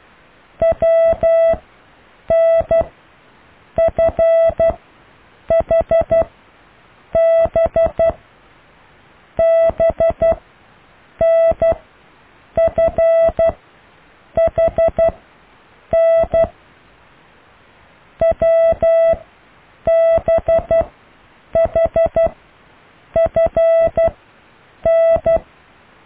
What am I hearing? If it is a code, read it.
Morse code, heard as a keyed tone, about 12 wpm, spaced as for 7 wpm, 660 Hz. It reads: WNFHB BNFHN WBHFN